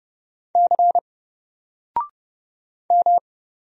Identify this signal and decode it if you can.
Morse code, heard as a keyed tone, about 30 wpm, 700 Hz. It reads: C M
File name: Single-Letter-Number-30wpm